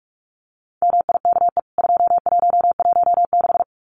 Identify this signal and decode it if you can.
Morse code, heard as a keyed tone, about 44 wpm, 700 Hz. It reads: MIKE 2116
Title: CWT Exchange 45wpm